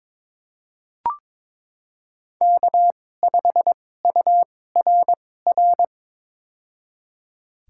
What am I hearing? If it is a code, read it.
Morse code, heard as a keyed tone, about 21 wpm, spaced as for 16 wpm, 700 Hz. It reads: K5URR
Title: USA Call Signs - 5 Characters - 2x Character Spacing - 22wpm